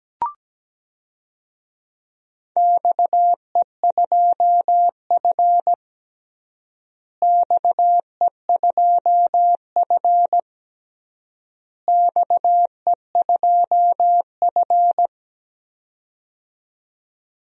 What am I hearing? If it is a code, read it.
Morse code, heard as a keyed tone, about 17 wpm, 700 Hz. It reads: XE2F XE2F XE2F
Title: DXCC-100 Call Signs - 4 Characters - Repeated 3x - 17wpm